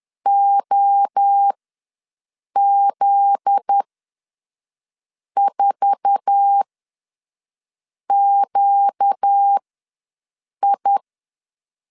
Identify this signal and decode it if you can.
Morse code, heard as a keyed tone, about 10 wpm, spaced as for 5 wpm, 800 Hz. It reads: OZ4QI
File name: LEZIONI / 8 YJ45 / 8 YJ4 ws 10wpm 4020